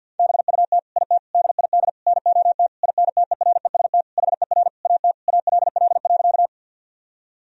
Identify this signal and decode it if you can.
Morse code, heard as a keyed tone, about 50 wpm, 700 Hz. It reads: BUT IT DID NOT INTEREST HER AT ALL.
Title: Sentences from Top 1000 Words - Review Part 2 - 50wpm